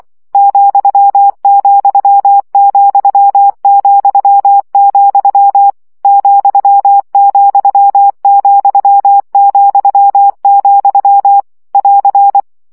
Morse code, heard as a keyed tone, about 24 wpm, 800 Hz. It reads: ,,,,, ,,,,, <AR>
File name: CW gyakorlas vesszo 20 baud